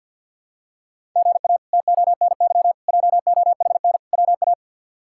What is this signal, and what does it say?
Morse code, heard as a keyed tone, about 46 wpm, 700 Hz. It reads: GA TONY JOHN WA
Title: K1USN SST QSO Practice - 50wpm